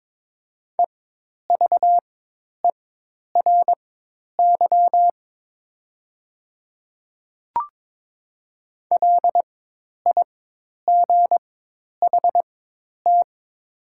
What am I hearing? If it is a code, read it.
Morse code, heard as a keyed tone, about 21 wpm, spaced as for 10 wpm, 700 Hz. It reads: EVERY LIGHT
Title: 4x Character Spacing - Top 200 Words - 22wpm